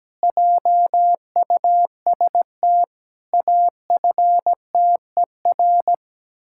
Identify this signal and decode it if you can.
Morse code, heard as a keyed tone, about 17 wpm, 700 Hz. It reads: JUST AFTER
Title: Pangram Challenge v2 - 17wpm